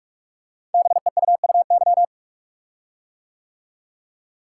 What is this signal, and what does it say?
Morse code, heard as a keyed tone, about 44 wpm, 700 Hz. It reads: BEUUY
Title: Code Groups - 5 Letters - 45wpm